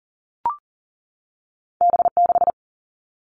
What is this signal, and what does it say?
Morse code, heard as a keyed tone, about 42 wpm, 700 Hz. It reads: B 6
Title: Two Challenging Letter-Number Combinations 40wpm